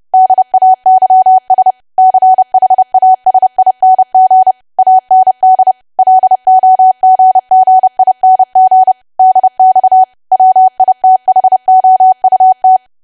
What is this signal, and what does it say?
Morse code, heard as a keyed tone, about 30 wpm, 750 Hz. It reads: DAYS CHASING AND LOGGING DX WITHOUT
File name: cwa / W1AW-Level3 / 30WPM